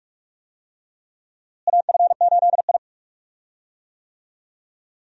Morse code, heard as a keyed tone, about 46 wpm, 700 Hz. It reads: AF8I
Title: USA Call Signs - 4 Characters - Repeated 3x - 45wpm